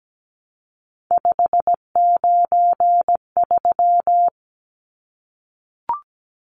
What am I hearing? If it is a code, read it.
Morse code, heard as a keyed tone, about 17 wpm, 700 Hz. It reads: 593
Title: Numbers - 3-Digit 17wpm